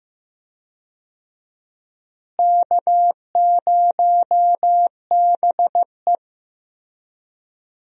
Morse code, heard as a keyed tone, about 16 wpm, 700 Hz. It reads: K0BE
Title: USA Call Signs - 4 Characters - Repeated 3x - 15wpm